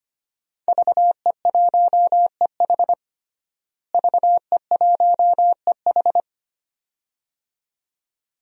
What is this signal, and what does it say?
Morse code, heard as a keyed tone, about 25 wpm, 700 Hz. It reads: VE1EH VE1EH
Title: DXCC-100 Call Signs - 5 Characters - Repeated 2x - 25wpm